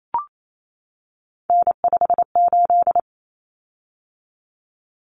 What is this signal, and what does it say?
Morse code, heard as a keyed tone, about 28 wpm, 700 Hz. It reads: N58
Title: Common Cut Numbers - 3-Digit 28wpm